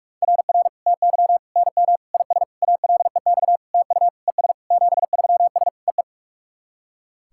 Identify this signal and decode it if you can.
Morse code, heard as a keyed tone, about 41 wpm, 690 Hz. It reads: RR TY NM IS ALEX TU ES 73S EE
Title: SOTA QSO Practice 45wpm